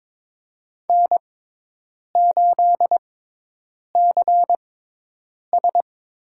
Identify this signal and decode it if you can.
Morse code, heard as a keyed tone, about 22 wpm, spaced as for 8 wpm, 700 Hz. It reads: N8CS
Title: USA Call Signs - 5 Characters - 6x Character Spacing - 22wpm